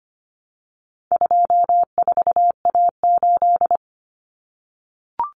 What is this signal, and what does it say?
Morse code, heard as a keyed tone, about 25 wpm, 700 Hz. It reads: 24A8
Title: Common Cut Numbers - 4-Digit 25wpm